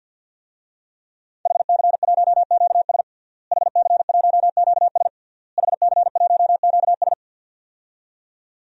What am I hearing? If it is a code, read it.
Morse code, heard as a keyed tone, about 51 wpm, 700 Hz. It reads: HC1QS HC1QS HC1QS
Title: DXCC-100 Call Signs - 5 Characters - Repeated 3x - 50wpm